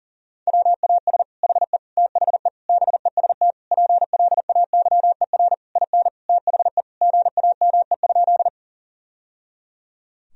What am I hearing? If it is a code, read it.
Morse code, heard as a keyed tone, about 40 wpm, 700 Hz. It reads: WAS HE THE BEST PLAYER IN THE GAME?
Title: Sentences from Top 1000 Words - Review Part 4 - 40wpm